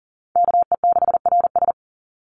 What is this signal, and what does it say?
Morse code, heard as a keyed tone, about 40 wpm, 700 Hz. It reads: KE6RS
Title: NA SOTA Chaser Callsigns 40wpm